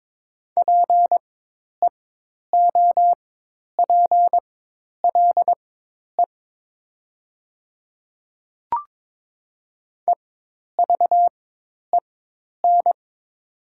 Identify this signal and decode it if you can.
Morse code, heard as a keyed tone, about 22 wpm, spaced as for 10 wpm, 700 Hz. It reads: PEOPLE EVEN
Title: 4x Character Spacing - Top 300 Words - 22wpm